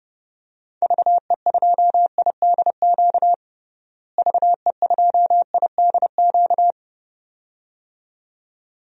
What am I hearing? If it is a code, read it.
Morse code, heard as a keyed tone, about 30 wpm, 700 Hz. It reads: VE2IDQ VE2IDQ
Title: DXCC-100 Call Signs - 6 Characters - Repeated 2x - 30wpm